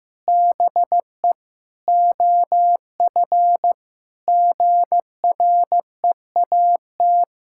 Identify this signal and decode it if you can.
Morse code, heard as a keyed tone, about 16 wpm, 700 Hz. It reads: BE OF GREAT
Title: Sentences from Top 1000 Words - Review Part 6 - 15wpm